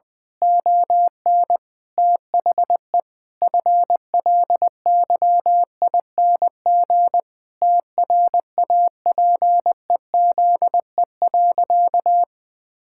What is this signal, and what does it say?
Morse code, heard as a keyed tone, about 20 wpm, 700 Hz. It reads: ON THE FLYING TRAPEZE.
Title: Pangram Challenge v2 - 20wpm